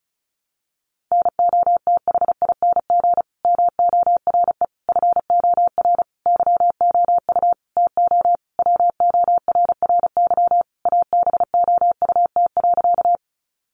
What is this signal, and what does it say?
Morse code, heard as a keyed tone, about 34 wpm, 700 Hz. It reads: NOTHING MORE FOR YOU TO WORRY ABOUT.
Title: Sentences from Top 1100 Words 35wpm